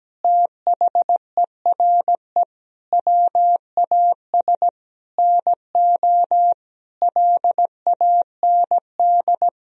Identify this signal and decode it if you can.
Morse code, heard as a keyed tone, about 17 wpm, 700 Hz. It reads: THERE WAS NO LAND
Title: Sentences from Top 1000 Words - Review Part 1 - 17wpm